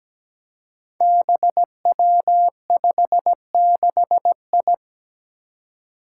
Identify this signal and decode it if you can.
Morse code, heard as a keyed tone, about 17 wpm, 700 Hz. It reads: BW56I
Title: Five Challenging Letter-Number Combinations 17wpm